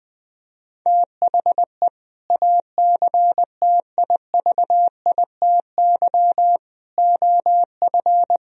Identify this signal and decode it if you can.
Morse code, heard as a keyed tone, about 21 wpm, 700 Hz. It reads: THE ACTIVITY OF
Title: Sentences from Top 1000 Words - Review Part 4 - 20wpm